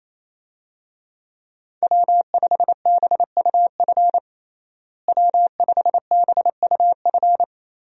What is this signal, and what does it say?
Morse code, heard as a keyed tone, about 28 wpm, 700 Hz. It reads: W5BUF W5BUF
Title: USA Call Signs - 5 Characters - Repeated 3x - 28wpm